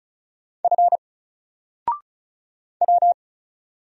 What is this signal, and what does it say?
Morse code, heard as a keyed tone, about 35 wpm, 700 Hz. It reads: F W
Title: Single Character (Letter:Number:Punctuation:Prosign) 35wpm